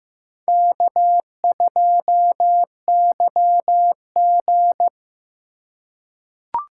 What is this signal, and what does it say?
Morse code, heard as a keyed tone, about 15 wpm, 700 Hz. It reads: K2YG
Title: USA Call Signs - 4x Character Spacing - 15wpm